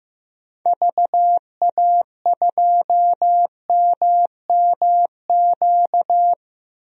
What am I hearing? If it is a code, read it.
Morse code, heard as a keyed tone, about 15 wpm, 700 Hz. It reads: VA2MMQ
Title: DXCC-100 Call Signs - 6 Characters - 15wpm